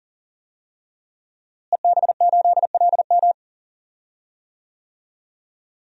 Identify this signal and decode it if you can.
Morse code, heard as a keyed tone, about 40 wpm, 700 Hz. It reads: EB8LM
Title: DXCC-100 Call Signs - 5 Characters - Repeated 3x - 40wpm